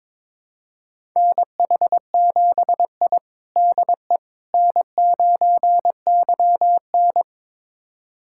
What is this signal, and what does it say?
Morse code, heard as a keyed tone, about 22 wpm, 700 Hz. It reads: NH7I DE N9YN
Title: USA Callsign DE Callsign - 4 Characters - 22wpm